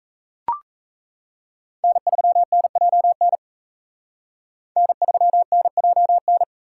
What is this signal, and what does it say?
Morse code, heard as a keyed tone, about 38 wpm, 700 Hz. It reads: N3NJN N3NJN